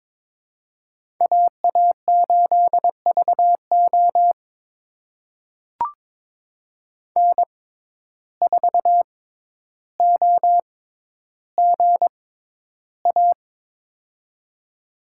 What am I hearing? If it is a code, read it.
Morse code, heard as a keyed tone, about 22 wpm, spaced as for 11 wpm, 700 Hz. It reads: AA8VO N4OGA
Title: USA Call Signs - 5 Characters - 6x Character Spacing - 22wpm